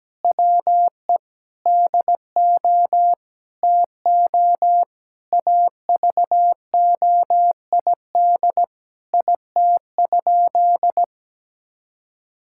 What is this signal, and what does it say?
Morse code, heard as a keyed tone, about 17 wpm, 700 Hz. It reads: WE DO TO AVOID IT?
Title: Sentences from Top 1000 Words - Review Part 6 - 17wpm